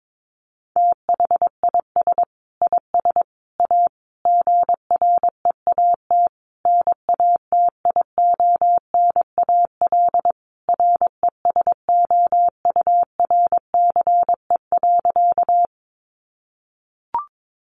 Morse code, heard as a keyed tone, about 22 wpm, 700 Hz. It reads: THIS IS A GREAT NATIONAL RESOURCE.
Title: Sentences from Top 1300 Words 22wpm